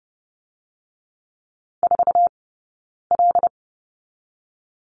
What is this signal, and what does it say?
Morse code, heard as a keyed tone, about 30 wpm, 700 Hz. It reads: 4 L
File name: Lesson 17d - Letter u - Two Characters 30wpm